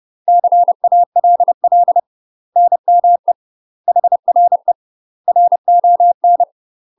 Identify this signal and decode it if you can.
Morse code, heard as a keyed tone, about 30 wpm, 700 Hz. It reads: CALL NME HRE RON
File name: UK MASTER QSOs TRAINING / 30WPM / UK MASTER QSOs 0020